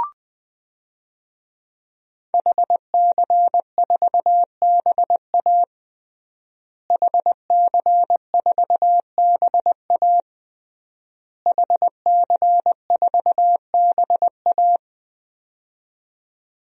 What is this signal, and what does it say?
Morse code, heard as a keyed tone, about 20 wpm, 700 Hz. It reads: HC4BA HC4BA HC4BA
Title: DXCC-100 Call Signs - 5 Characters - Repeated 3x - 20wpm